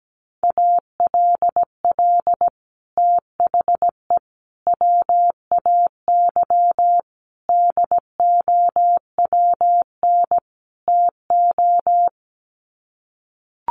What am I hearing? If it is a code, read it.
Morse code, heard as a keyed tone, about 17 wpm, 700 Hz. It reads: ALL THE WAY DOWN TO
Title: Sets of 5 Words Encore2 17wpm